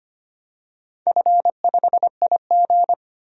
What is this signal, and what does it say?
Morse code, heard as a keyed tone, about 25 wpm, 700 Hz. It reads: F5IG
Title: DXCC-100 Call Signs - 4 Characters - 25wpm